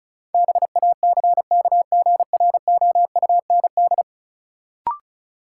Morse code, heard as a keyed tone, about 34 wpm, 700 Hz. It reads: BACKGROUND
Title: Sentences from Top 1300 Words - New Word Drill - 35wpm